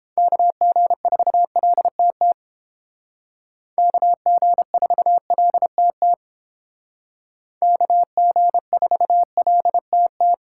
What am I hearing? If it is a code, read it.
Morse code, heard as a keyed tone, about 31 wpm, 700 Hz. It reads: KG4LTT KG4LTT KG4LTT